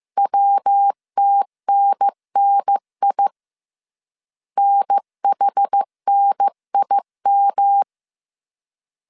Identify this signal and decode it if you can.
Morse code, heard as a keyed tone, about 15 wpm, 800 Hz. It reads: WTNNI NHNIM
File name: CW / DET / 1 3 1500C S A 14wpm 2447